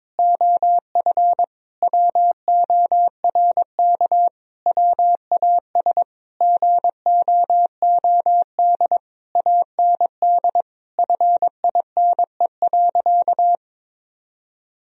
Morse code, heard as a keyed tone, about 22 wpm, 700 Hz. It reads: OF WORK WAS GOOD AND FINE.
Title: Sentences from Top 1000 Words - Review Part 2 - 22wpm